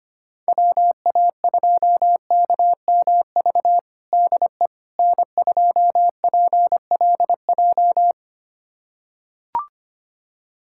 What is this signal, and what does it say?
Morse code, heard as a keyed tone, about 25 wpm, 700 Hz. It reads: WA2KMV DE N2PLJ
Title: USA Callsign DE Callsign 25wpm